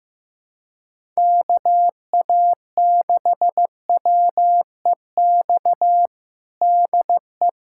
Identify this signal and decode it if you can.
Morse code, heard as a keyed tone, about 15 wpm, 700 Hz. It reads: KA6WEX DE
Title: USA Callsign DE Callsign - 6 Characters - 15wpm